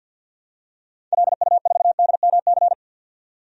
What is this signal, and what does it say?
Morse code, heard as a keyed tone, about 51 wpm, 700 Hz. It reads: LU4DMC
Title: DXCC-100 Call Signs - 6 Characters - 50wpm